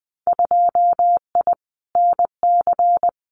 Morse code, heard as a keyed tone, about 20 wpm, 700 Hz. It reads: 2I NC
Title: Winter Field Day Exchanges 20wpm